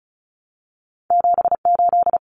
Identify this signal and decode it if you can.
Morse code, heard as a keyed tone, about 36 wpm, 700 Hz. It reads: 78
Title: Common Cut Numbers - 2-Digit 35wpm